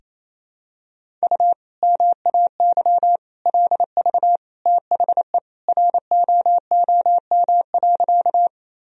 Morse code, heard as a keyed tone, about 28 wpm, 700 Hz. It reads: U MAY LV THE ROOM.
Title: Sentences from Top 400 Words and Abbreviations 28wpm